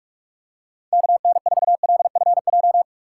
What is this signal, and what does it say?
Morse code, heard as a keyed tone, about 46 wpm, 700 Hz. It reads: KN4LFJ
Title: USA Call Signs - 6 Characters - 2x Character Spacing - 45wpm